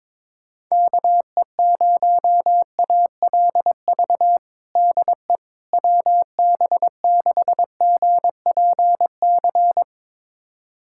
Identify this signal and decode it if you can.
Morse code, heard as a keyed tone, about 22 wpm, 700 Hz. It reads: KE0ALV DE WB6GPC
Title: USA Callsign DE Callsign - 6 Characters - 22wpm